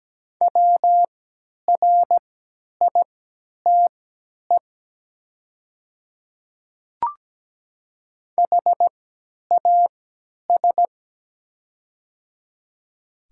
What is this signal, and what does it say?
Morse code, heard as a keyed tone, about 17 wpm, spaced as for 10 wpm, 700 Hz. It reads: WRITE HAS
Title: 3x Character Spacing - Top 100 Words - 17wpm